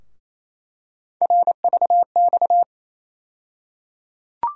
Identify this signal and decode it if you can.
Morse code, heard as a keyed tone, about 28 wpm, 700 Hz. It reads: RVX
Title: USA Call Sign Postfixes - 3 Letters 28wpm